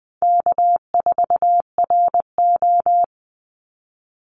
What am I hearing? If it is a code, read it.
Morse code, heard as a keyed tone, about 20 wpm, 700 Hz. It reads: K4RO
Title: Call Signs Repeated 3x - 20wpm